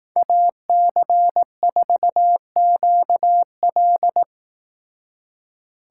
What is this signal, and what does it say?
Morse code, heard as a keyed tone, about 18 wpm, 700 Hz. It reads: AC4QL
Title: Speed Racing - USA 2x2 Call Signs 15wpm